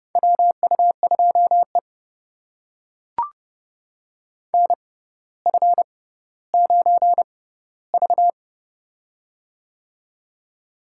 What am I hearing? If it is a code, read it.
Morse code, heard as a keyed tone, about 30 wpm, spaced as for 15 wpm, 700 Hz. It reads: WU2E NF9V
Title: USA Call Signs - 4 Characters - 6x Character Spacing - 30wpm